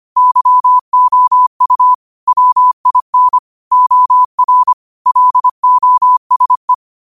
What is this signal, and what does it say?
Morse code, heard as a keyed tone, about 25 wpm, 1000 Hz. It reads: YOU WIN OR LOSE
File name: qotdfiles / 25wpm